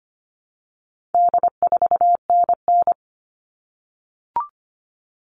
Morse code, heard as a keyed tone, about 25 wpm, 700 Hz. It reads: D4NN
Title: All Cut Numbers - 4-Digit 25wpm